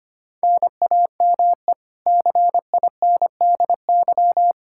Morse code, heard as a keyed tone, about 25 wpm, 700 Hz. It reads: NAME CINDY
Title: K1USN SST QSO Practice 25wpm